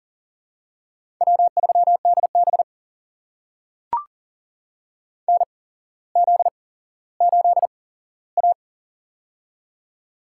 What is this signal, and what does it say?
Morse code, heard as a keyed tone, about 37 wpm, spaced as for 17 wpm, 700 Hz. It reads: W3DB NZ8A
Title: USA Call Signs - 8x Character Spacing - 40wpm